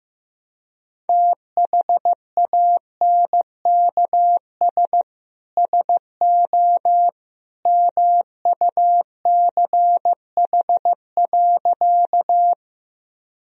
Morse code, heard as a keyed tone, about 15 wpm, 700 Hz. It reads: THANKS SO MUCH.
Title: Sentences from Top 1000 Words - Review Part 3 - 15wpm